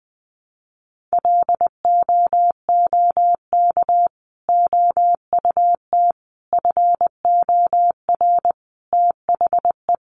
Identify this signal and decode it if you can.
Morse code, heard as a keyed tone, about 20 wpm, 700 Hz. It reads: LOOK OUT FOR THE
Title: Sentences from Top 900 Words 20wpm